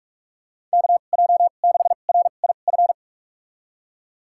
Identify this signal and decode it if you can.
Morse code, heard as a keyed tone, about 42 wpm, spaced as for 31 wpm, 690 Hz. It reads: KJ6RIF
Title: USA Call Signs - 6 Characters - 2x Character Spacing - 45wpm